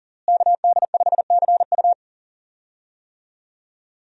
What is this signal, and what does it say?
Morse code, heard as a keyed tone, about 40 wpm, 700 Hz. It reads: KD5CU
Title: USA Call Signs - 5 Characters - Repeated 3x - 40wpm